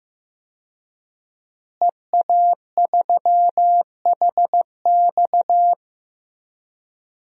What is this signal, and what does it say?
Morse code, heard as a keyed tone, about 14 wpm, 700 Hz. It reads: EA3HX
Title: DXCC-100 Call Signs - 5 Characters - Repeated 3x - 15wpm